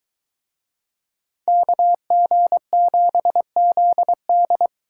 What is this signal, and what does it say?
Morse code, heard as a keyed tone, about 22 wpm, 700 Hz. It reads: KG7ZD